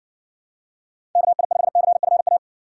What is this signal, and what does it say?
Morse code, heard as a keyed tone, about 60 wpm, 700 Hz. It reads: KI5CFA